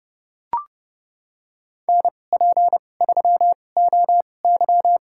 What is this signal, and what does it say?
Morse code, heard as a keyed tone, about 31 wpm, spaced as for 22 wpm, 700 Hz. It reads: NP3OY
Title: USA Call Signs - 5 Characters - 2x Character Spacing - 30wpm